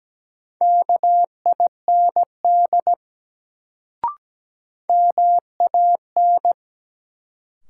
Morse code, heard as a keyed tone, about 17 wpm, 700 Hz. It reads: KIND MAN
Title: Sets of 2 Words - New Word Drill - 17wpm